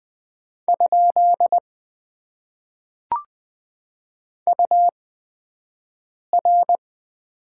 Morse code, heard as a keyed tone, about 20 wpm, spaced as for 5 wpm, 700 Hz. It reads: ? UR
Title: Top 90 QSO Elements - 8x Character Spacing - 20wpm